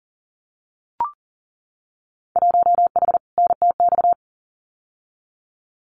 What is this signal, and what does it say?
Morse code, heard as a keyed tone, about 40 wpm, 700 Hz. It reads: 1H NTX
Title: Winter Field Day Exchanges 40wpm